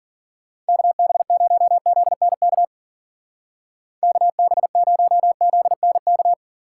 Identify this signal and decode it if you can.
Morse code, heard as a keyed tone, about 51 wpm, 700 Hz. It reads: KB0ZNK KB0ZNK